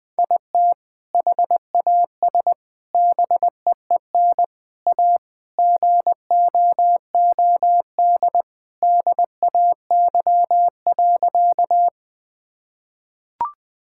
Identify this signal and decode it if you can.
Morse code, heard as a keyed tone, about 20 wpm, 700 Hz. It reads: IT HAS BEEN A GOOD DAY.
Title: Sentences with 2-Word Phrases - 20wpm